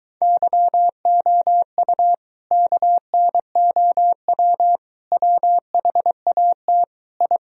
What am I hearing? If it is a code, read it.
Morse code, heard as a keyed tone, about 24 wpm, 700 Hz. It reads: YOU KNOW WHAT I